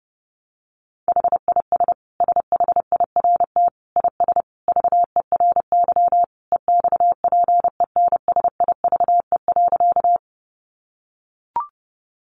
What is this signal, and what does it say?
Morse code, heard as a keyed tone, about 30 wpm, 700 Hz. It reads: HIS SHIRT IS VERY EXPENSIVE.
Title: Additional Easy Phrases 30wpm